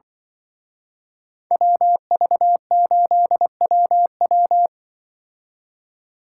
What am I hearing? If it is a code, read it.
Morse code, heard as a keyed tone, about 25 wpm, 700 Hz. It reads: WV8WW
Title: Speed Racing - USA 2x2 Call Signs 20wpm